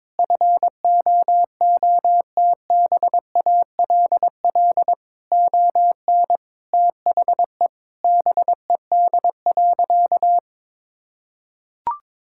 Morse code, heard as a keyed tone, about 22 wpm, 700 Hz. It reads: FOOTBALL ON THE BED.
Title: Sentences from Top 1600 Words 22wpm